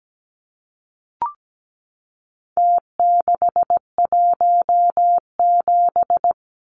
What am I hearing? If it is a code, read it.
Morse code, heard as a keyed tone, about 17 wpm, 700 Hz. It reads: T617
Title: Numbers - Contest Style - 4-Digit 17wpm